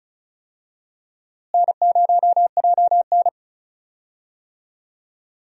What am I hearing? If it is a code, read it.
Morse code, heard as a keyed tone, about 36 wpm, 700 Hz. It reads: N0JN
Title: USA Call Signs - 4 Characters - Repeated 2x - 35wpm